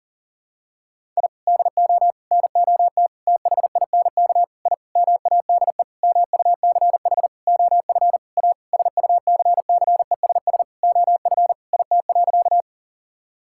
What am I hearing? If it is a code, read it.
Morse code, heard as a keyed tone, about 40 wpm, 700 Hz. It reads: I DO NOT THINK I MADE MUCH OF A SUCCESS OF IT.
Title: Sentences from Top 1000 Words - Review Part 6 - 40wpm